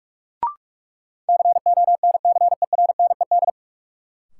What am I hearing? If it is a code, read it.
Morse code, heard as a keyed tone, about 46 wpm, 700 Hz. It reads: CONCERNED
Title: Sentences from Top 1000 Words - Review Part 6 - New Word Drill - 45wpm